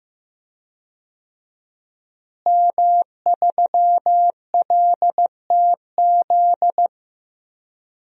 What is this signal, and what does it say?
Morse code, heard as a keyed tone, about 15 wpm, 700 Hz. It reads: M3LTZ
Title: DXCC-100 Call Signs - 5 Characters - Repeated 3x - 15wpm